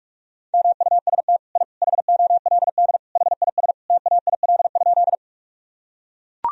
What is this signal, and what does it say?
Morse code, heard as a keyed tone, about 41 wpm, 700 Hz. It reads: MUST I HOLD HIS TAIL?
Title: Sentences from Top 1000 Words - Review Part 3 - 45wpm